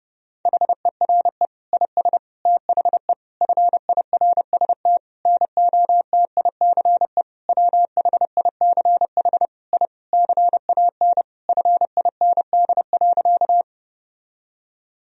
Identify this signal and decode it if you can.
Morse code, heard as a keyed tone, about 32 wpm, 700 Hz. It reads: HERE IS THE FIRST NOTICE WHICH I CAN FIND.
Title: Sentences from Top 1000 Words - Review Part 2 - 30wpm